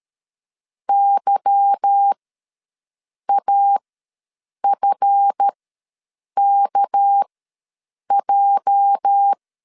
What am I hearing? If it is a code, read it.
Morse code, heard as a keyed tone, about 13 wpm, spaced as for 7 wpm, 800 Hz. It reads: YAFKJ
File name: LEZIONI / 10 SIGNS / 10B INT ws 12wpm 3527